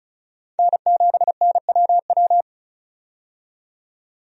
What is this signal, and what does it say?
Morse code, heard as a keyed tone, about 36 wpm, 700 Hz. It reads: N7NWW
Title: USA Call Signs - 5 Characters - 35wpm